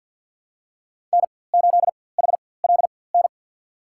Morse code, heard as a keyed tone, about 52 wpm, spaced as for 23 wpm, 700 Hz. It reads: N8HLN
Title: USA Call Signs - 4x Character Spacing - 50wpm